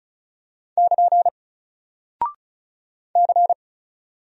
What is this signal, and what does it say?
Morse code, heard as a keyed tone, about 36 wpm, 700 Hz. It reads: <KN> C
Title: Single Character (Letter:Number:Punctuation:Prosign) Alt 35wpm